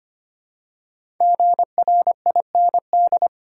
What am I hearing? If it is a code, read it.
Morse code, heard as a keyed tone, about 25 wpm, 700 Hz. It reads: GRIND
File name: 5 Letter Words - First 3 Letters Given - 25wpm